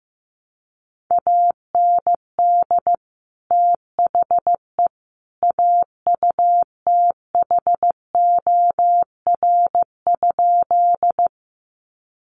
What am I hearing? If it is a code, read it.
Morse code, heard as a keyed tone, about 15 wpm, 700 Hz. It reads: AND THE AUTHOR?
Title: Sentences from Top 1200 Words 15wpm